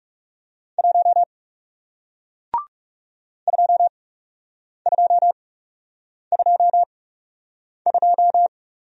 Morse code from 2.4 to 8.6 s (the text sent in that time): 2222